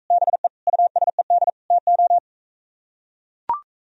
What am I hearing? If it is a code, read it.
Morse code, heard as a keyed tone, about 44 wpm, 700 Hz. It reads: BE USED TO